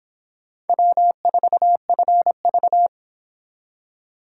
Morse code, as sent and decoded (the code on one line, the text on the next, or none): .-- ....- ..-. ...-
W4FV